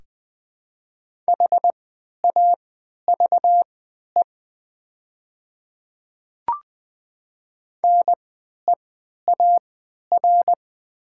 Morse code, heard as a keyed tone, about 20 wpm, spaced as for 11 wpm, 700 Hz. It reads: HAVE NEAR
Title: 3x Character Spacing - Top 300 Words - 20wpm